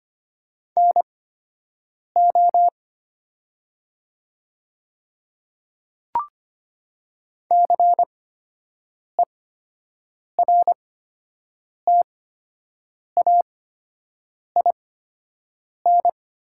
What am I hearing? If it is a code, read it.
Morse code, heard as a keyed tone, about 25 wpm, spaced as for 7 wpm, 700 Hz. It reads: NO CERTAIN
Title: 8x Character Spacing - Top 500 Words - 25wpm